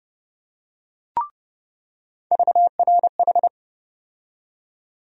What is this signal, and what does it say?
Morse code, heard as a keyed tone, about 30 wpm, 700 Hz. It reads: VRH
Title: USA Call Sign Postfixes - 3 Letters 30wpm